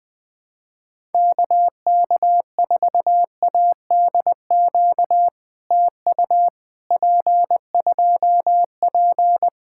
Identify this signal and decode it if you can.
Morse code, heard as a keyed tone, about 20 wpm, 700 Hz. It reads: KK4ADQ TU P2P
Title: POTA QSO Practice - 20wpm